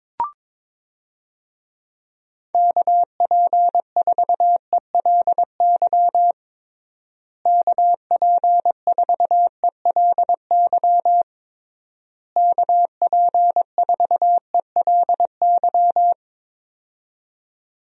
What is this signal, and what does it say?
Morse code, heard as a keyed tone, about 22 wpm, 700 Hz. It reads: KP4ELY KP4ELY KP4ELY
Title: USA Call Signs - 6 Characters - Repeated 3x - 22wpm